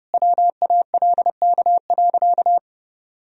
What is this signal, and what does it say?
Morse code, heard as a keyed tone, about 30 wpm, 700 Hz. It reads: WALK.
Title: Sentences from Top 1000 Words - Review Part 2 - 30wpm